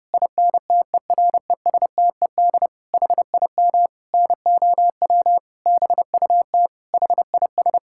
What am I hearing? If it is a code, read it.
Morse code, heard as a keyed tone, about 30 wpm, 700 Hz. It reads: INTERESTED HIM NOW BUT HIS
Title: Sentences from Top 1000 Words - Review Part 6 - 30wpm